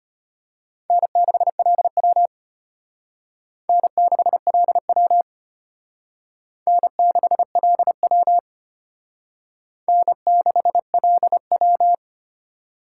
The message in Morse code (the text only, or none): N6LW N6LW N6LW N6LW